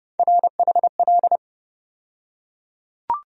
Morse code, heard as a keyed tone, about 30 wpm, 700 Hz. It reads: RHL
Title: USA Call Sign Postfixes - 3 Letters 30wpm